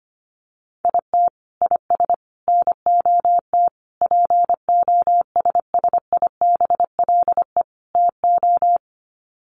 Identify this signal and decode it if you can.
Morse code, heard as a keyed tone, about 25 wpm, 700 Hz. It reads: IT IS NOT POSSIBLE TO
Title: Sets of 5 Words Encore2 25wpm